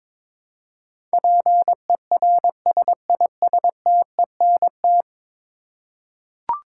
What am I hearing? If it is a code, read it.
Morse code, heard as a keyed tone, about 22 wpm, 700 Hz. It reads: PERSISTENT
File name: practice / 22wpm / 10 Letter Words - First 3 Letters Given - 22wpm